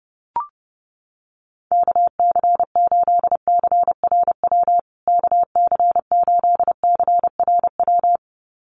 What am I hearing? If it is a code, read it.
Morse code, heard as a keyed tone, about 30 wpm, 700 Hz. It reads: KC8CRW KC8CRW
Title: USA Call Signs Repeated 2x 30wpm